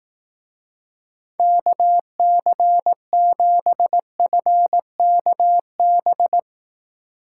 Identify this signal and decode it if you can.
Morse code, heard as a keyed tone, about 19 wpm, 700 Hz. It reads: KC7FKB
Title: Speed Racing - USA Call Signs 15wpm